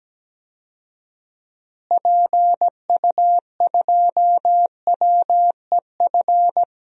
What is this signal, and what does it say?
Morse code, heard as a keyed tone, about 17 wpm, 700 Hz. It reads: PU2WEF
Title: DXCC-100 Call Signs - 6 Characters - Repeated 3x - 17wpm